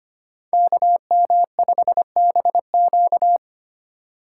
Morse code, heard as a keyed tone, about 25 wpm, 700 Hz. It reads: KM5BQ
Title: USA Call Signs - 5 Characters - 8x Character Spacing - 25wpm